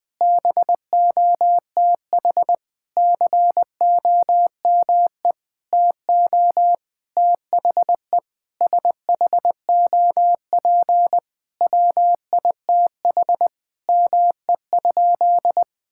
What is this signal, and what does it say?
Morse code, heard as a keyed tone, about 20 wpm, 700 Hz. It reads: BOTH COME TO THE SHOP WITH ME?
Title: Sentences from Top 1600 Words 20wpm